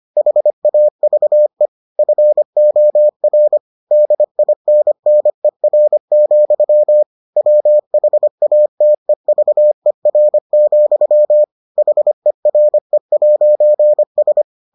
Morse code, heard as a keyed tone, about 25 wpm, 600 Hz. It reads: HAVE FOR DINNER, WHATEVER, HERE'S